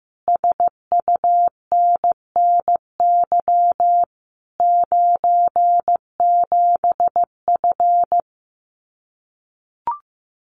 Morse code, heard as a keyed tone, about 15 wpm, 700 Hz. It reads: SUNNY 97F
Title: Common QSO Phrases Encore 15wpm